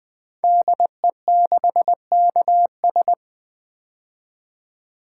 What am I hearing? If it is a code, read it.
Morse code, heard as a keyed tone, about 20 wpm, 700 Hz. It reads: DE6KS
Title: Code Groups - 5 Letter:Numbers - 20wpm